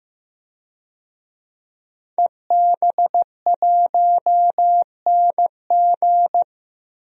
Morse code, heard as a keyed tone, about 15 wpm, 700 Hz. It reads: EB1NG
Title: DXCC-100 Call Signs - 5 Characters - 15wpm